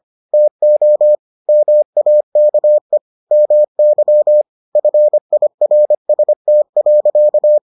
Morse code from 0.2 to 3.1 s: TO MAKE

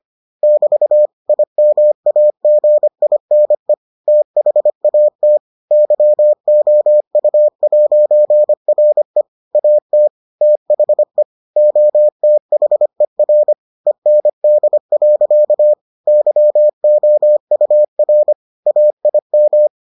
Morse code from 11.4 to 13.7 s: OTHER

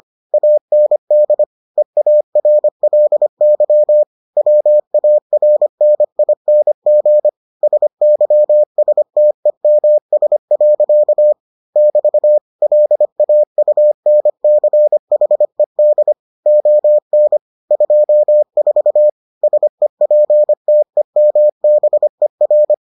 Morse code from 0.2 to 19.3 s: AND EARLY WARNING SYSTEMS. <BT> LAUNCHED ON 24

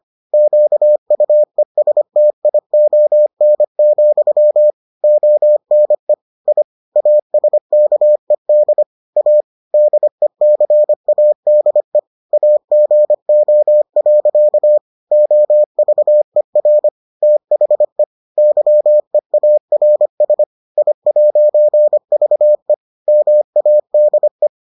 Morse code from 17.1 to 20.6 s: THE YEARS

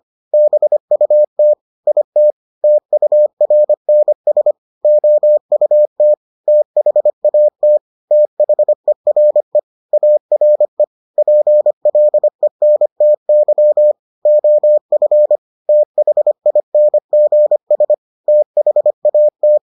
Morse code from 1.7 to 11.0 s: IT TURNS OUT THAT THERE ARE

